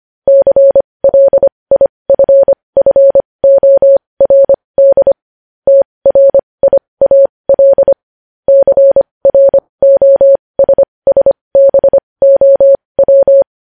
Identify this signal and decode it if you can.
Morse code, heard as a keyed tone, about 25 wpm, spaced as for 20 wpm, 550 Hz. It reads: CLIFFORD TRIAL CROSSBOW
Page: News Headlines in Morse Code at 20 WPM